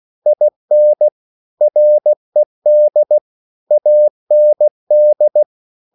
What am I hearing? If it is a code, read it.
Morse code, heard as a keyed tone, about 16 wpm, 600 Hz. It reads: IN RED AND